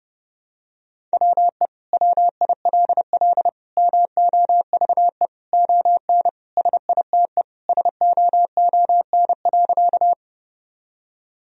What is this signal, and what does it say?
Morse code, heard as a keyed tone, about 32 wpm, 700 Hz. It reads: WE WILL MOVE ON SITE SOON.
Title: Sentences from Top 1000 Words - Review Part 4 - 30wpm